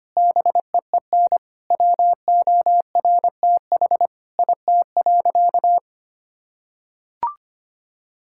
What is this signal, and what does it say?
Morse code, heard as a keyed tone, about 25 wpm, 700 Hz. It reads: BEEN WORTH IT.
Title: Sentences from Top 1000 Words - Review Part 5 - 25wpm